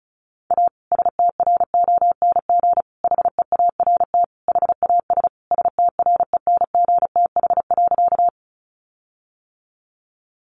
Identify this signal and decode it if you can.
Morse code, heard as a keyed tone, about 34 wpm, 700 Hz. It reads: A STRONG HEART HAS STRENGTH.
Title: Sentences from Top 1300 Words 35wpm